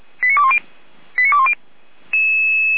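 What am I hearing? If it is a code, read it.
Five-tone selective call (ZVEI), heard as quick runs of five tones, tones of about 70 ms. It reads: 89210 89210
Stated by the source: ZVEI five-tone call 89117 received on SDR.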